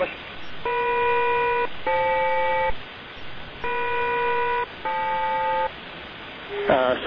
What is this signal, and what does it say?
Aircraft SELCAL, heard as two pairs of tones, with speech. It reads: EM-FL EM-FL